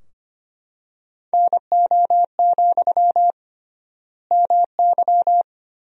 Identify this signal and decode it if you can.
Morse code, heard as a keyed tone, about 25 wpm, 700 Hz. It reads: NO, MY
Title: 3x Word-Spacing - Sentences from Top 400 Words - 25wpm